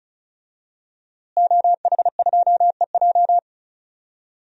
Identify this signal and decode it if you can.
Morse code, heard as a keyed tone, about 36 wpm, 700 Hz. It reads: OH2EJ
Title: DXCC-100 Call Signs - 5 Characters - Repeated 3x - 35wpm